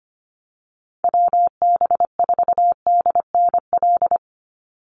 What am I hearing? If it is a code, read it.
Morse code, heard as a keyed tone, about 25 wpm, 700 Hz. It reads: WB4DNL
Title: Call Signs - 25wpm